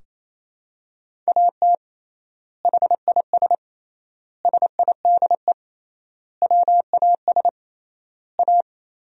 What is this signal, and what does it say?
Morse code, heard as a keyed tone, about 28 wpm, 700 Hz. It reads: AT HIS SIDE WAS A
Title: 3x Word-Spacing - Sentences from Top 500 Words - 28wpm